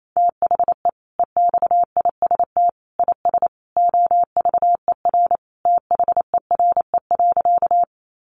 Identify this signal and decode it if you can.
Morse code, heard as a keyed tone, about 28 wpm, 700 Hz. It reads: THE EXIST IS OVER THERE.
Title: Sentences from Top 1100 Words 28wpm